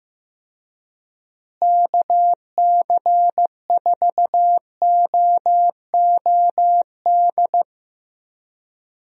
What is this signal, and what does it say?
Morse code, heard as a keyed tone, about 14 wpm, 700 Hz. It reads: KC4OOD
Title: USA Call Signs - 6 Characters - Repeated 3x - 15wpm